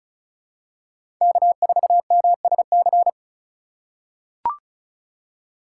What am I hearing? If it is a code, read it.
Morse code, heard as a keyed tone, about 36 wpm, 700 Hz. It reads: K4MSC
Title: USA Call Signs - 5 Characters - 2x Character Spacing - 35wpm